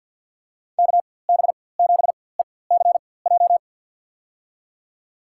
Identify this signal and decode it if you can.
Morse code, heard as a keyed tone, about 46 wpm, spaced as for 23 wpm, 700 Hz. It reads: KB7ECJ
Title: USA Call Signs - 4x Character Spacing - 50wpm